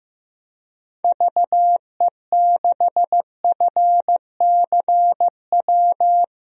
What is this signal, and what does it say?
Morse code, heard as a keyed tone, about 15 wpm, 700 Hz. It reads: VE6FCW
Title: DXCC-100 Call Signs - 6 Characters - 15wpm